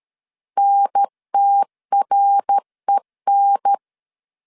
Morse code, heard as a keyed tone, about 13 wpm, 800 Hz. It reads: NTREN
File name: CW / DET / 1 3 1500C S C 12wpm 2750